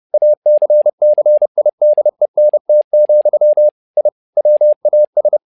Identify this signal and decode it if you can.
Morse code, heard as a keyed tone, about 30 wpm, 600 Hz. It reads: ACCIDENT, I WAS
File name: learning center / tower trouble / tower trouble 30wpm